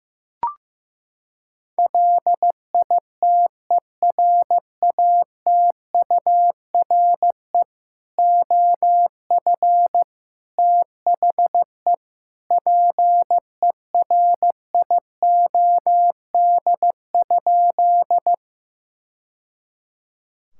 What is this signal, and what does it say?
Morse code, heard as a keyed tone, about 15 wpm, 700 Hz. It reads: LITERATURE OF THE PERIOD?
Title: Sentences from Top 1000 Words - Review Part 5 - 15wpm